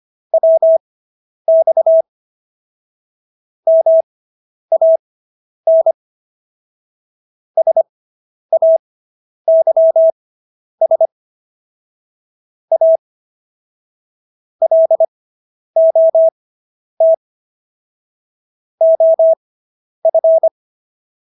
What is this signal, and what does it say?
Morse code, heard as a keyed tone, about 25 wpm, spaced as for 10 wpm, 650 Hz. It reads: WX MAN SAYS A LOT OF